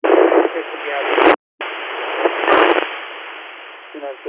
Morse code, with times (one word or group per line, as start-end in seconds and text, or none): none